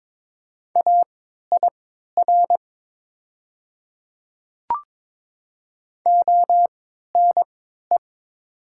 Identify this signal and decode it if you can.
Morse code, heard as a keyed tone, about 22 wpm, spaced as for 12 wpm, 700 Hz. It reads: AIR ONE
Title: 3x Character Spacing - Top 200 Words - 22wpm